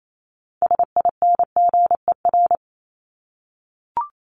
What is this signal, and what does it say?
Morse code, heard as a keyed tone, about 28 wpm, 700 Hz. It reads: SINGER
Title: Top 1501-1750 Words 28wpm